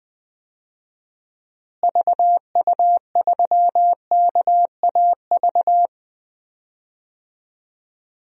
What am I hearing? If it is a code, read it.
Morse code, heard as a keyed tone, about 20 wpm, 700 Hz. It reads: VU3KAV
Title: DXCC-100 Call Signs - 6 Characters - Repeated 3x - 20wpm